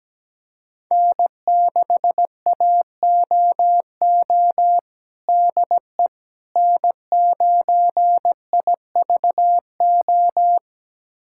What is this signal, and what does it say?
Morse code, heard as a keyed tone, about 17 wpm, 700 Hz. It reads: N6AOO DE N9IVO
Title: USA Callsign DE Callsign - 5 Characters - 17wpm